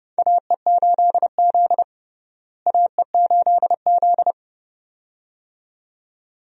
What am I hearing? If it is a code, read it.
Morse code, heard as a keyed tone, about 30 wpm, 700 Hz. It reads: AE8Z AE8Z
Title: USA Call Signs - 4 Characters - Repeated 2x - 30wpm